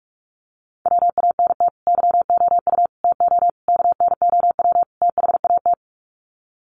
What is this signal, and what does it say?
Morse code, heard as a keyed tone, about 44 wpm, 700 Hz. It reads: WANT YOU TO KNOW THAT
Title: Sets of 5 Words Encore 45wpm